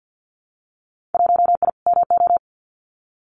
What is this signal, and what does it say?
Morse code, heard as a keyed tone, about 50 wpm, 700 Hz. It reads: 1I MO
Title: Winter Field Day Exchanges Encore 50wpm